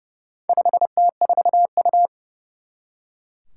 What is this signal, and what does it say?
Morse code, heard as a keyed tone, about 29 wpm, 700 Hz. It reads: 5T4U
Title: Four Challenging Letter-Number Combinations 30wpm